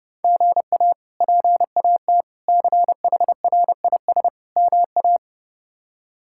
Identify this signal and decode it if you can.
Morse code, heard as a keyed tone, about 32 wpm, 700 Hz. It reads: GA PAT CHRIS MA
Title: K1USN SST QSO Practice - 30wpm